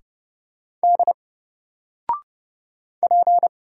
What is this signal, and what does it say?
Morse code, heard as a keyed tone, about 30 wpm, 700 Hz. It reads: D P
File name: Single-Letter-Number-30wpm